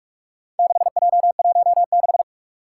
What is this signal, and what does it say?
Morse code, heard as a keyed tone, about 44 wpm, 700 Hz. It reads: 6J16
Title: Four Challenging Letter-Number Combinations 45wpm